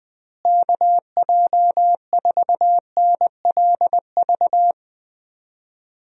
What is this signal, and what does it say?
Morse code, heard as a keyed tone, about 21 wpm, 700 Hz. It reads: KJ4NLV
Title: USA Call Signs - 6 Characters - Repeated 2x - 20wpm